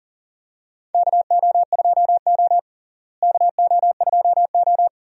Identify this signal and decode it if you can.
Morse code, heard as a keyed tone, about 37 wpm, 700 Hz. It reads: KO2O KO2O
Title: USA Call Signs - 4 Characters - Repeated 3x - 40wpm